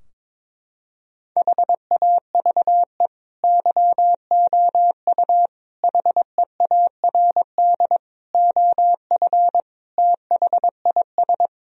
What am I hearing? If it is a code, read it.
Morse code, heard as a keyed tone, about 22 wpm, 700 Hz. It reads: HAVE YOU HEARD OF THIS
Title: Sentences from Top 1000 Words - Review Part 3 - 22wpm